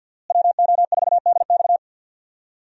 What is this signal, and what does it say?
Morse code, heard as a keyed tone, about 53 wpm, 700 Hz. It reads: WO4DX
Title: USA Call Signs - 5 Characters - Repeated 2x - 50wpm